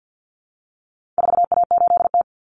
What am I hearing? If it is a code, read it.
Morse code, heard as a keyed tone, about 50 wpm, 700 Hz. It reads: 4A8T
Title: Common Cut Numbers - 4-Digit 50wpm